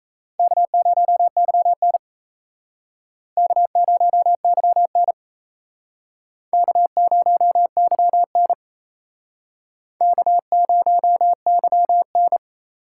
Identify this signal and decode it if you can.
Morse code, heard as a keyed tone, about 34 wpm, 700 Hz. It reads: K0YN K0YN K0YN K0YN
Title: Speed Racing - USA 1x2 Call Signs 28wpm